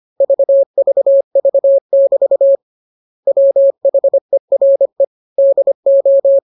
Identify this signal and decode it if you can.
Morse code, heard as a keyed tone, about 25 wpm, 550 Hz. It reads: VVV<BT> WHERE DO
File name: jokes / joke 19 25 0000